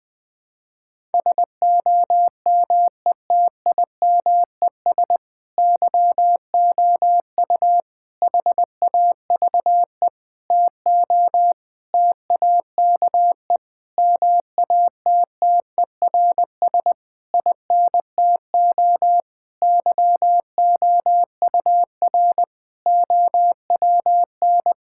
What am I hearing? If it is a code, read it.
Morse code, heard as a keyed tone, about 20 wpm, 700 Hz. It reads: SOMETIMES YOU HAVE TO TAKE MATTERS INTO YOUR OWN
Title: Sentences with 7-Word Phrases - 20wpm